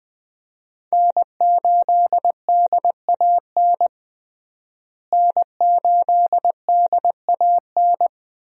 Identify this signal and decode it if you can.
Morse code, heard as a keyed tone, about 21 wpm, 700 Hz. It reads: N8DAN N8DAN
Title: USA Call Signs - 5 Characters - Repeated 2x - 20wpm